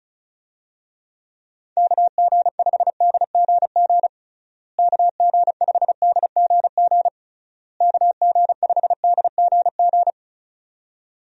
Morse code, heard as a keyed tone, about 34 wpm, 700 Hz. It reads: KG5DGG KG5DGG KG5DGG
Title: USA Call Signs - 6 Characters - Repeated 3x - 35wpm